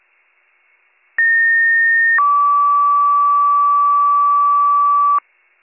Two-tone page, heard as a one-second tone, then a three-second tone, about 1850 then 1150 Hz, with noise. It is heard from 1.2 to 5.2 s.